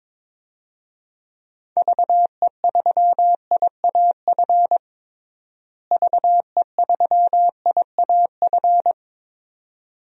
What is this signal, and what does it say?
Morse code, heard as a keyed tone, about 22 wpm, 700 Hz. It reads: VE3IAF VE3IAF
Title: DXCC-100 Call Signs - 6 Characters - Repeated 2x - 22wpm